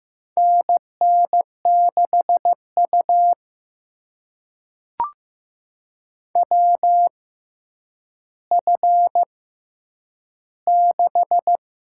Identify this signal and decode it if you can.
Morse code, heard as a keyed tone, about 15 wpm, spaced as for 9 wpm, 700 Hz. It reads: NN6U WF6
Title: USA Call Signs - 4 Characters - 6x Character Spacing - 15wpm